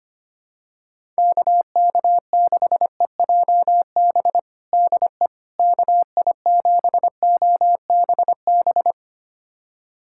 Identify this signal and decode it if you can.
Morse code, heard as a keyed tone, about 25 wpm, 700 Hz. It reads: KK6EJB DE KI7OBB
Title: USA Callsign DE Callsign - 6 Characters - 25wpm